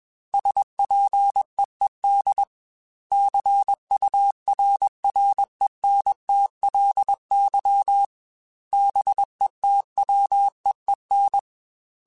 Morse code, heard as a keyed tone, about 21 wpm, 800 Hz. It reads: SPEED CURRENTLY BETWEEN
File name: Weather Sets / Weather Sets-mp3s / Weather 5 12 @20